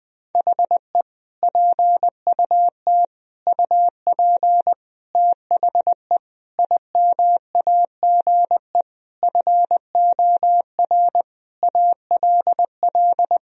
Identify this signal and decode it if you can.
Morse code, heard as a keyed tone, about 21 wpm, 700 Hz. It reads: HE PUT UP THE IMAGE FOR ALL
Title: Sentences from Top 1000 Words - Review Part 4 - 20wpm